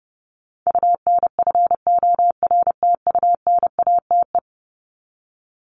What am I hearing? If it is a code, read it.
Morse code, heard as a keyed tone, about 30 wpm, 700 Hz. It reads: UNFORTUNATE
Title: LY Suffix - Top 100 Words with Base Words 30wpm